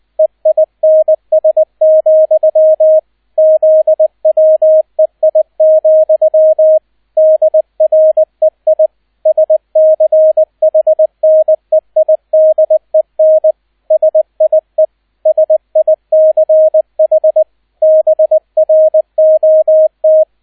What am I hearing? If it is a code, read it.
Morse code, heard as a keyed tone, about 20 wpm, 630 Hz, with radio noise. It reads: EINS, ZWEI, DREI SCHNEIDEN SIE SICH BROT
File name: Erster Streich 20wpm